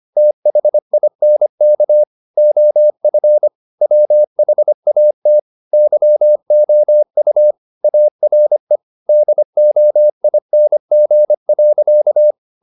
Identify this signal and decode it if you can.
Morse code, heard as a keyed tone, about 25 wpm, 600 Hz. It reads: THINK OF WHAT YOU ARE DOING.
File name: the old man and the sea part 1 25wpm